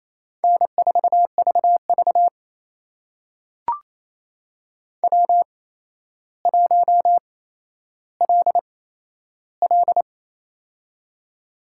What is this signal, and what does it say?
Morse code, heard as a keyed tone, about 28 wpm, spaced as for 12 wpm, 700 Hz. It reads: N4VV W1LL
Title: USA Call Signs - 4 Characters - 8x Character Spacing - 28wpm